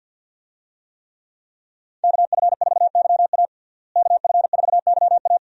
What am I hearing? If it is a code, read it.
Morse code, heard as a keyed tone, about 52 wpm, 700 Hz. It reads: KF4YA KF4YA
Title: USA Call Signs - 5 Characters - Repeated 3x - 50wpm